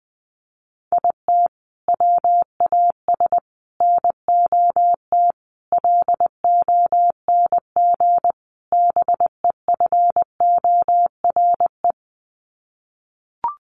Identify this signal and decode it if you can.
Morse code, heard as a keyed tone, about 20 wpm, 700 Hz. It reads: IT WAS NOT LONG BEFORE
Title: Sets of 5 Words Encore2 20wpm